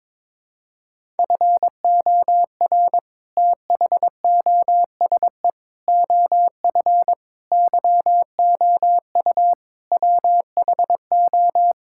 Morse code, heard as a keyed tone, about 22 wpm, 700 Hz. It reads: FOR THOSE OF YOU WHO
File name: practice / 22wpm / Sets of 6 Words - 22wpm